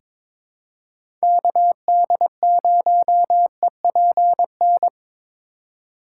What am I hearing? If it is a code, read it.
Morse code, heard as a keyed tone, about 22 wpm, 700 Hz. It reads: KD0EPN
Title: USA Call Signs - 8x Character Spacing - 22wpm